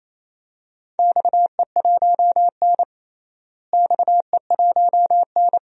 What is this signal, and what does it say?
Morse code, heard as a keyed tone, about 28 wpm, 700 Hz. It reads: XE1N XE1N
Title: DXCC-100 Call Signs - 4 Characters - Repeated 2x - 28wpm